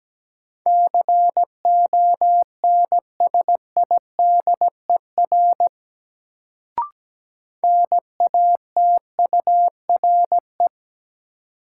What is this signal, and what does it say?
Morse code, heard as a keyed tone, about 17 wpm, 700 Hz. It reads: CONSIDER NATURE
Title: Sentences from Top 600 Words - New Word Drill - 17wpm